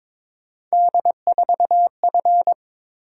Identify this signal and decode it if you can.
Morse code, heard as a keyed tone, about 27 wpm, 700 Hz. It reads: D4F
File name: Speed-Racing - Three Letter-Number Combinations - 22wpm